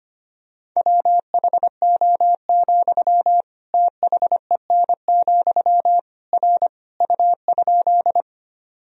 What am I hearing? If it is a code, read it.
Morse code, heard as a keyed tone, about 25 wpm, 700 Hz. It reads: WHO, THEN, R U?
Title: Sentences from Top 100 Words and Abbreviations 25wpm